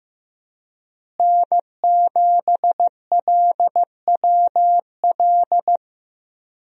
Morse code, heard as a keyed tone, about 15 wpm, 700 Hz. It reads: N7LWL
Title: USA Call Signs - 5 Characters - Repeated 3x - 15wpm